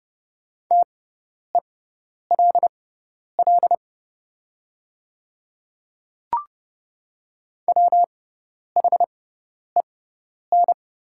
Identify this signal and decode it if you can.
Morse code, heard as a keyed tone, about 29 wpm, spaced as for 10 wpm, 690 Hz. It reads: TELL WHEN
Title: 6x Character Spacing - Top 300 Words - 30wpm